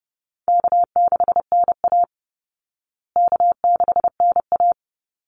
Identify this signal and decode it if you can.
Morse code, heard as a keyed tone, about 30 wpm, 700 Hz. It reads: K6NA K6NA
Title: Easy Call Signs Repeated 3x - 30wpm